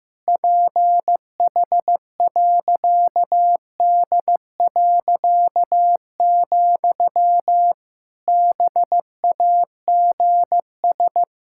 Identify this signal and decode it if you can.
Morse code, heard as a keyed tone, about 15 wpm, 700 Hz. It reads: PH.D., BAGS
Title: Pangram Challenge v2 - 15wpm